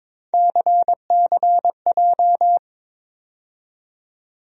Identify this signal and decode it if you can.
Morse code, heard as a keyed tone, about 22 wpm, 700 Hz. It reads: CCJ
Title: USA Call Sign Postfixes - 3 Letters 22wpm